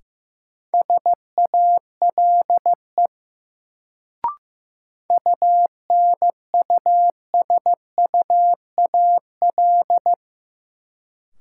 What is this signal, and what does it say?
Morse code, heard as a keyed tone, about 15 wpm, 700 Hz. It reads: SALE UNUSUAL
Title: Sentences from Top 1400 Words - New Word Drill - 15wpm